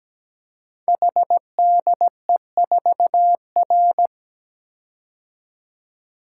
Morse code, heard as a keyed tone, about 17 wpm, 700 Hz. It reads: HDE4R
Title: Code Groups - 5 Letter:Numbers - 17wpm